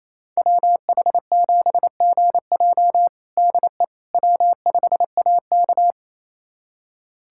Morse code, heard as a keyed tone, about 28 wpm, 700 Hz. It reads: WH7GJ DE W5AK
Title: USA Callsign DE Callsign 28wpm